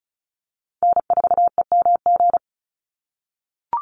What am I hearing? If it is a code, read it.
Morse code, heard as a keyed tone, about 34 wpm, 700 Hz. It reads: N4EMG
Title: NA SOTA Chaser Callsigns 35wpm